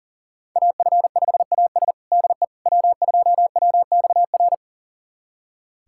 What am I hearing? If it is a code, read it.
Morse code, heard as a keyed tone, about 40 wpm, 700 Hz. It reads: AF5AS DE W2WXR
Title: USA Callsign DE Callsign - 5 Characters - 40wpm